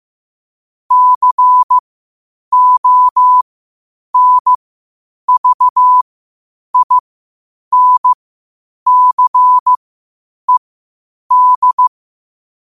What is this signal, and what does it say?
Morse code, heard as a keyed tone, about 15 wpm, spaced as for 8 wpm, 1000 Hz. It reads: CONVINCED
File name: qotdfiles / 8wpm